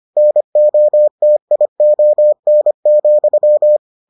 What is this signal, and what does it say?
Morse code, heard as a keyed tone, about 25 wpm, 600 Hz. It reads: NOTION,